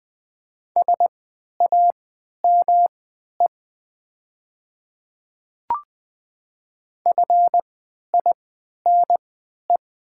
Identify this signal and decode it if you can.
Morse code, heard as a keyed tone, about 20 wpm, spaced as for 11 wpm, 700 Hz. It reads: SAME FINE
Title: 3x Character Spacing - Top 500 Words - 20wpm